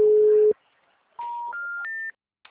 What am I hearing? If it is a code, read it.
Special information tone.
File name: out- -2005